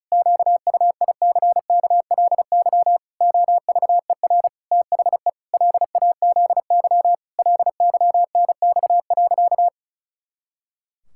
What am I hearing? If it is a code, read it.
Morse code, heard as a keyed tone, about 34 wpm, 700 Hz. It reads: QUICKLY OVER THE LAZY LYNX.
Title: Pangram Challenge v2 - 35wpm